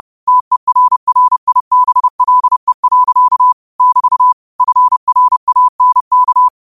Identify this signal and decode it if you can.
Morse code, heard as a keyed tone, about 30 wpm, 1000 Hz. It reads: TERRIBLE. <BT> FRANK
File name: qotdfiles / 30wpm